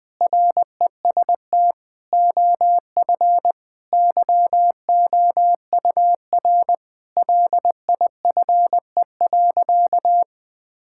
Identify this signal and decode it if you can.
Morse code, heard as a keyed tone, about 20 wpm, 700 Hz. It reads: REST OF YOUR LIFE.
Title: Sentences with 5-Word Phrases - Spoken Beginning - 20wpm